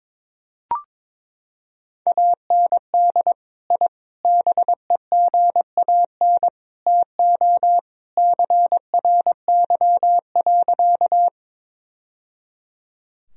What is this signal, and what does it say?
Morse code, heard as a keyed tone, about 22 wpm, 700 Hz. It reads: AND I BEGAN TO CRY.
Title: Sentences from Top 1000 Words - Review Part 2 - 22wpm